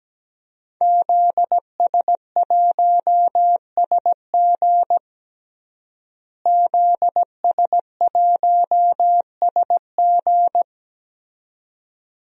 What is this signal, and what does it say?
Morse code, heard as a keyed tone, about 17 wpm, 700 Hz. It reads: ZS1SG ZS1SG
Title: DXCC-100 Call Signs - 5 Characters - Repeated 2x - 17wpm